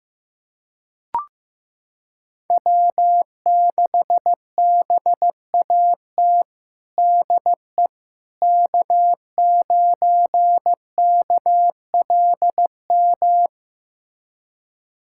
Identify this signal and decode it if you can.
Morse code, heard as a keyed tone, about 15 wpm, 700 Hz. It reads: W6BAT DE K9KLM
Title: USA Callsign DE Callsign - 5 Characters - 15wpm